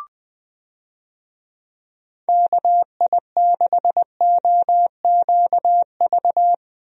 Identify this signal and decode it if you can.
Morse code, heard as a keyed tone, about 21 wpm, 700 Hz. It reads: KI6OQV
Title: USA Call Signs - 6 Characters - Repeated 2x - 20wpm